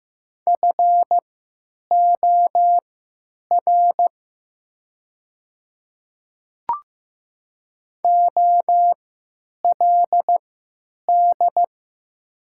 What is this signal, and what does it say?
Morse code, heard as a keyed tone, about 15 wpm, spaced as for 9 wpm, 700 Hz. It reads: FOR OLD
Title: 3x Character Spacing - Top 200 Words - 15wpm